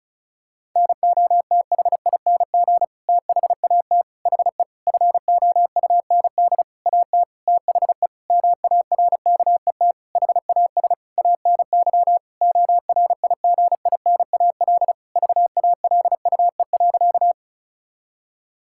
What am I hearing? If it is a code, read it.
Morse code, heard as a keyed tone, about 36 wpm, 700 Hz. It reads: NOTHING THAT HE FOUND AT THE MARKET HAS ANY ORIGINAL VALUE.
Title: Sentences from Top 1000 Words - Review Part 6 - 35wpm